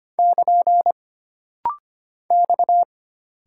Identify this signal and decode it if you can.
Morse code, heard as a keyed tone, about 25 wpm, 700 Hz. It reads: <KN> X
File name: Single Character (Letter:Number:Punctuation:Prosign) - Rapid-Fire - 25wpm